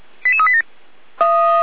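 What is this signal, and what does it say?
Five-tone selective call (ZVEI), heard as quick runs of five tones, tones of about 70 ms. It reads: 88387
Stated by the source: ZVEI five-tone selective call received with an SDR.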